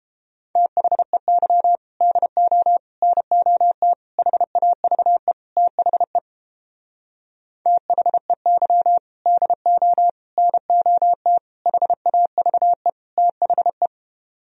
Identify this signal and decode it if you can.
Morse code, heard as a keyed tone, about 31 wpm, 700 Hz. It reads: THEY DO NOT HAVE THE THEY DO NOT HAVE THE